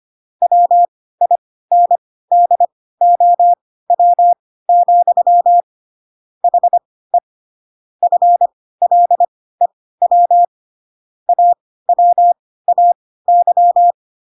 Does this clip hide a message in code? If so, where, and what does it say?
Yes: Morse code from 0.4 to 13.9 s, heard as a keyed tone, about 25 wpm, spaced as for 16 wpm, 700 Hz. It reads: WINDOW, HE FLEW AWAY